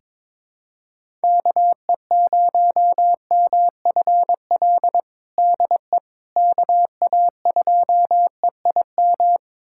Morse code, heard as a keyed tone, about 22 wpm, 700 Hz. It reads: KE0MFL DE KA2EIM
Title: USA Callsign DE Callsign - 6 Characters - 22wpm